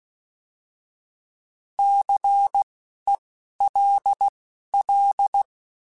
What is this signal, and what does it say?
Morse code, heard as a keyed tone, about 16 wpm, spaced as for 12 wpm, 800 Hz. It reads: CELL
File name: Alphabetical / Alphabetical mp3s / Cs @10